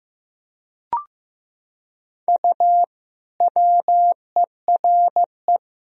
Morse code, heard as a keyed tone, about 16 wpm, 700 Hz. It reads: U WERE
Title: Sentences from Top 400 Words and Abbreviations 15wpm